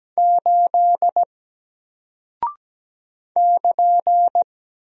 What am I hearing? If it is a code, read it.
Morse code, heard as a keyed tone, about 17 wpm, 700 Hz. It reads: 8 <KN>
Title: Single Character (Letter:Number:Punctuation:Prosign) Alt 17wpm